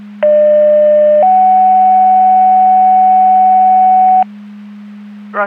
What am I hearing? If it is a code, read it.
Two-tone page, heard as a one-second tone, then a three-second tone, about 600 then 750 Hz. It is heard from 0.2 to 4.2 s.
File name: Portage 154.385 Paging